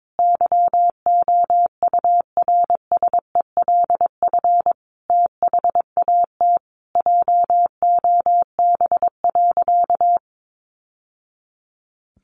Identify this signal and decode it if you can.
Morse code, heard as a keyed tone, about 22 wpm, 700 Hz. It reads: YOURSELF THAT JOB.
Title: Sentences from Top 1100 Words 22wpm